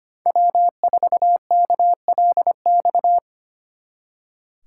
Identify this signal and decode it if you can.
Morse code, heard as a keyed tone, about 33 wpm, 700 Hz. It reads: W4KLX